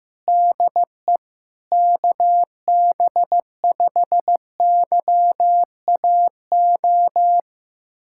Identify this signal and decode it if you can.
Morse code, heard as a keyed tone, about 16 wpm, 700 Hz. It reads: DE KB5YAO
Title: USA Callsign DE Callsign - 6 Characters - 15wpm